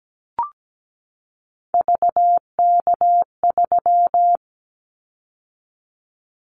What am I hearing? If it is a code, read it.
Morse code, heard as a keyed tone, about 17 wpm, 700 Hz. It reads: VK3
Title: Callsign Prefixes for Top 50 DXCC - 17wpm